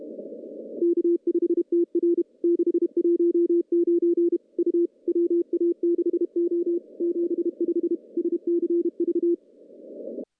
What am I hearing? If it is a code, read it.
Morse code, heard as a keyed tone, about 30 wpm, 340 Hz. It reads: K5TR 619 U WA6O 75 SCV
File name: contests / CW-SS-k5tr- -0700